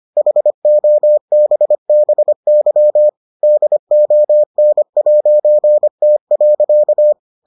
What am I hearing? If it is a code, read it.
Morse code, heard as a keyed tone, about 25 wpm, 600 Hz. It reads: HOBBY DON'T.